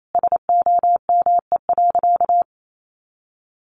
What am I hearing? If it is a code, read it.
Morse code, heard as a keyed tone, about 28 wpm, 700 Hz. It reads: SOME.
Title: Sentences from Top 100 Words 28wpm